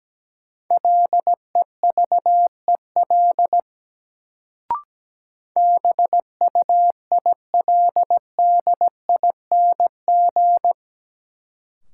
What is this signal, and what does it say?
Morse code, heard as a keyed tone, about 17 wpm, 700 Hz. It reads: LEVEL BUILDING
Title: Sentences from Top 700 Words - New Word Drill - 17wpm